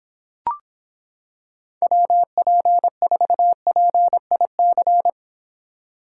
Morse code, heard as a keyed tone, about 26 wpm, 700 Hz. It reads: WP4PIC